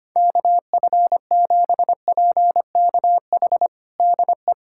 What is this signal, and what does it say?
Morse code, heard as a keyed tone, about 25 wpm, 700 Hz. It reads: KF7PKH DE
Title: USA Callsign DE Callsign 25wpm